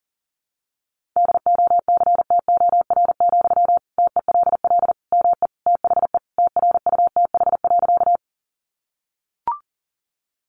Morse code, heard as a keyed tone, about 40 wpm, 700 Hz. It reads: DOCTOR, TELL ME THE TRUTH.
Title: Sentences from Top 1500 Words 40wpm